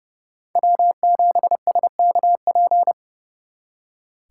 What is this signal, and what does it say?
Morse code, heard as a keyed tone, about 36 wpm, 700 Hz. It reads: W7SKP
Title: Speed Racing - USA Call Signs 30wpm